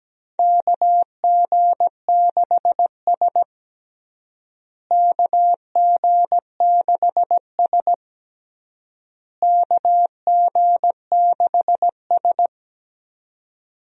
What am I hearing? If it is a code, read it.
Morse code, heard as a keyed tone, about 17 wpm, 700 Hz. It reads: KG6S KG6S KG6S
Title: USA Call Signs - 4 Characters - Repeated 3x - 17wpm